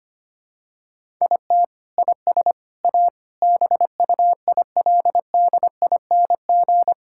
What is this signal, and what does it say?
Morse code, heard as a keyed tone, about 25 wpm, 700 Hz. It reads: IT IS A BUILDING
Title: Sentences from Top 1000 Words - Review Part 5 - 25wpm